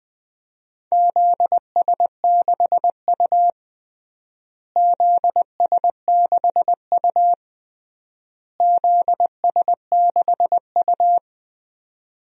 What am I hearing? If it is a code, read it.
Morse code, heard as a keyed tone, about 20 wpm, 700 Hz. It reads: ZS6U ZS6U ZS6U
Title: DXCC-100 Call Signs - 4 Characters - Repeated 3x - 20wpm